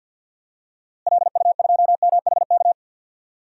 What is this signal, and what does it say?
Morse code, heard as a keyed tone, about 53 wpm, 700 Hz. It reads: LU2MHK
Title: DXCC-100 Call Signs - 6 Characters - Repeated 3x - 50wpm